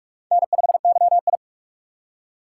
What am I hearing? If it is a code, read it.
Morse code, heard as a keyed tone, about 44 wpm, 700 Hz. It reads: N5YI
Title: USA Call Signs - 4 Characters - 6x Character Spacing - 45wpm